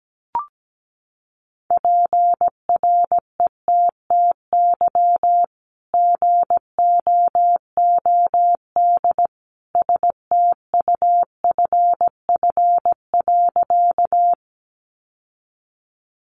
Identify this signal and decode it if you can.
Morse code, heard as a keyed tone, about 17 wpm, 700 Hz. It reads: PRETTY GOOD STUFF.
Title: Sentences from Top 1400 Words 17wpm